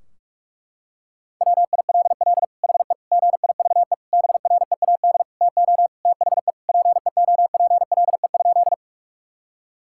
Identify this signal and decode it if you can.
Morse code, heard as a keyed tone, about 46 wpm, 700 Hz. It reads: WILL HE GIVE BREAD TO THE PEOPLE?
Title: Sentences from Top 1600 Words 45wpm